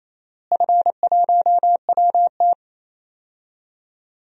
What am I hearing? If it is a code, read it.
Morse code, heard as a keyed tone, about 28 wpm, 700 Hz. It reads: F1WT
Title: DXCC-100 Call Signs - 4 Characters - Repeated 3x - 28wpm